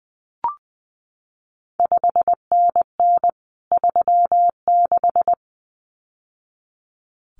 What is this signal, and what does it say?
Morse code, heard as a keyed tone, about 20 wpm, 700 Hz. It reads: 5NN 36
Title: CQ WW DX Contest Exchange Encore - 20wpm